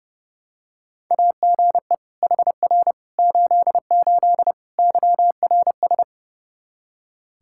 Morse code, heard as a keyed tone, about 30 wpm, 700 Hz. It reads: AGE HR 88 YRS
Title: Practice QSO - Exchange 3 - More Info - 30wpm